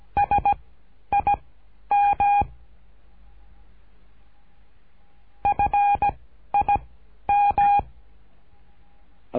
Morse code, heard as a keyed tone, about 17 wpm, spaced as for 11 wpm, 810 Hz. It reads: SIM FIM
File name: cursos / CURSO DE TELEGRAFIA / Telegrafia - Trilha 13